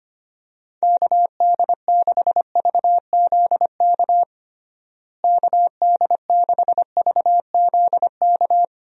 Morse code, heard as a keyed tone, about 25 wpm, 700 Hz. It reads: KD6VZK KD6VZK
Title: USA Call Signs - 6 Characters - Repeated 3x - 25wpm